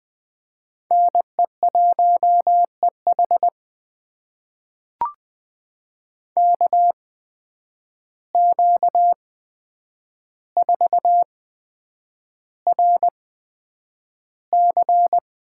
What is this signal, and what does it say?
Morse code, heard as a keyed tone, about 20 wpm, spaced as for 9 wpm, 700 Hz. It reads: NE1EH KQ4RC
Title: USA Call Signs - 5 Characters - 8x Character Spacing - 20wpm